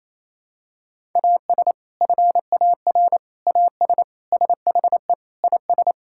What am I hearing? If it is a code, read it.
Morse code, heard as a keyed tone, about 28 wpm, 700 Hz. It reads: AS FAR AS SHE IS
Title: Sentences from Top 1000 Words - Review Part 6 - 28wpm